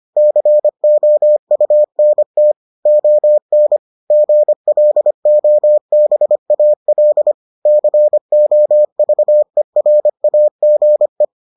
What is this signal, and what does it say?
Morse code, heard as a keyed tone, about 25 wpm, 600 Hz. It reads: COUNT ON GLOBAL COVERAGE